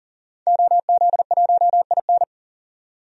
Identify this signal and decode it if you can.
Morse code, heard as a keyed tone, about 37 wpm, 700 Hz. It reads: OZ1IN
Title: DXCC-100 Call Signs - 5 Characters - 40wpm